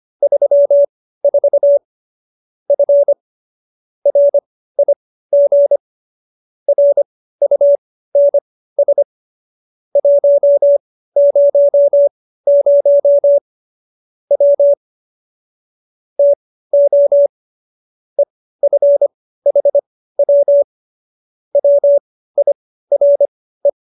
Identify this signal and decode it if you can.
Morse code, heard as a keyed tone, about 25 wpm, spaced as for 15 wpm, 570 Hz. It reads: 34 F RIG RUNS 100 W TO EFHW WIRE